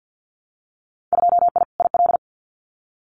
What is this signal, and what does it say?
Morse code, heard as a keyed tone, about 50 wpm, 700 Hz. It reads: 2I IL
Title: Winter Field Day Exchanges Encore 50wpm